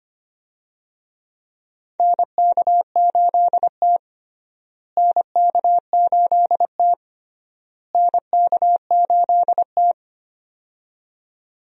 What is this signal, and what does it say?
Morse code, heard as a keyed tone, about 25 wpm, 700 Hz. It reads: NK8T NK8T NK8T
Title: USA Call Signs - 4 Characters - Repeated 3x - 25wpm